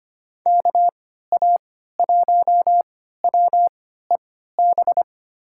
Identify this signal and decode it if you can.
Morse code, heard as a keyed tone, about 25 wpm, spaced as for 14 wpm, 700 Hz. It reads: KA1WEB
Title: USA Call Signs - 3x Character Spacing - 25wpm